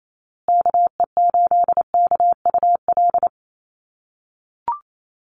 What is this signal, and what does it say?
Morse code, heard as a keyed tone, about 28 wpm, 700 Hz. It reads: KE8KUL
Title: USA Call Signs Repeated 3x 28wpm